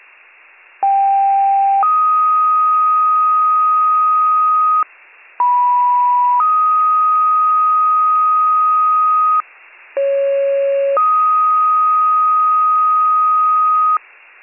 Two-tone page, heard as a one-second tone, then a three-second tone, about 790 then 1250 Hz, about 980 then 1250 Hz, about 550 then 1250 Hz. It is heard from 0.8 to 14.0 s.